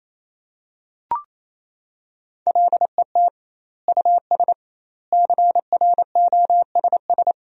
Morse code, heard as a keyed tone, about 28 wpm, 700 Hz. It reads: LET US CROSS
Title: 2x Word-Spacing - Sentences from Top 300 Words - 28wpm